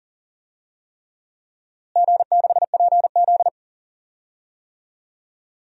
Morse code, heard as a keyed tone, about 37 wpm, 700 Hz. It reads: G6PZ
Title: DXCC-100 Call Signs - 4 Characters - Repeated 3x - 40wpm